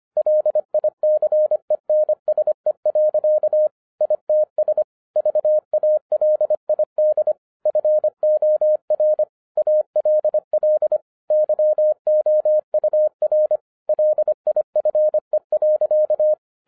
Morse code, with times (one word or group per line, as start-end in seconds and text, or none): 0.2-3.7 LICENSE.
4.0-4.8 ITS
5.2-7.3 VALID
7.7-9.2 FOR
9.6-11.0 ALL
11.3-13.6 YOUR
13.9-16.3 LIFE.